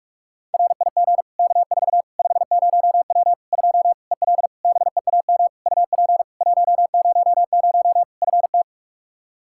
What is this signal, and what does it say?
Morse code, heard as a keyed tone, about 46 wpm, 700 Hz. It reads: RIG K4 50W 2 EL BEAM UP 100 FT
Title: Practice QSO - Exchange 2 - Info - 45wpm